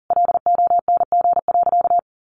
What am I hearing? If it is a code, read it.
Morse code, heard as a keyed tone, about 37 wpm, 700 Hz. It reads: LONG.
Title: Sentences from Top 1300 Words 40wpm